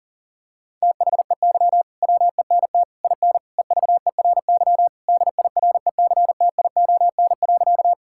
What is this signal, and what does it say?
Morse code, heard as a keyed tone, about 40 wpm, 700 Hz. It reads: THEY WENT IN EVERY DIRECTION.
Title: Sentences from Top 1000 Words - Review Part 5 - 40wpm